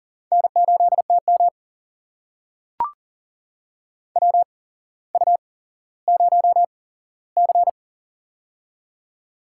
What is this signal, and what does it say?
Morse code, heard as a keyed tone, about 40 wpm, spaced as for 17 wpm, 700 Hz. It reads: N8TM WU0C
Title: USA Call Signs - 4 Characters - 8x Character Spacing - 40wpm